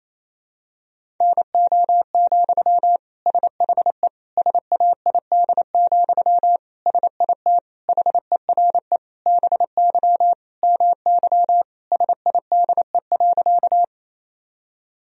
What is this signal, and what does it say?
Morse code, heard as a keyed tone, about 27 wpm, 700 Hz. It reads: NO, SHE SAID, SIT HERE BY MY SIDE.
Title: Sentences from Top 1000 Words - Review Part 1 - 28wpm